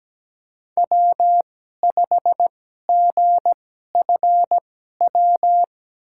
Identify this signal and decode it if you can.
Morse code, heard as a keyed tone, about 17 wpm, spaced as for 12 wpm, 700 Hz. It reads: W5GFW
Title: USA Call Signs - 2x Character Spacing - 17wpm